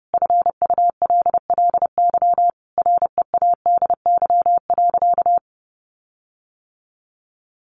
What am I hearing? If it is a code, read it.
Morse code, heard as a keyed tone, about 30 wpm, 700 Hz. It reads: FULLY READY.
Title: Sentences from Top 1300 Words 30wpm